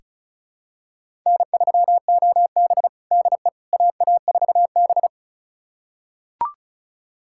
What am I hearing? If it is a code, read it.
Morse code, heard as a keyed tone, about 36 wpm, 700 Hz. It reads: N3OB DE AA4B
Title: USA Callsign DE Callsign - 4 Characters - 35wpm